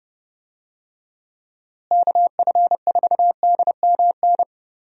Morse code, heard as a keyed tone, about 32 wpm, 700 Hz. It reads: KF4DMN
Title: USA Call Signs - 6 Characters - Repeated 3x - 30wpm